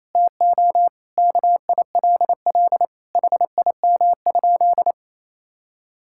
Morse code, heard as a keyed tone, about 28 wpm, 700 Hz. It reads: TO KILL HIM?
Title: Sentences from Top 1600 Words 28wpm